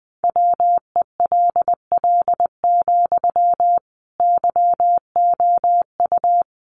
Morse code, heard as a keyed tone, about 20 wpm, 700 Hz. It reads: WELL, YOU
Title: Sentences from Top 700 Words 20wpm